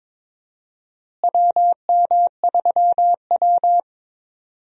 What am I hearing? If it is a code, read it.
Morse code, heard as a keyed tone, about 22 wpm, 700 Hz. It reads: WM3W
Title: USA Call Signs - 4 Characters - Repeated 2x - 22wpm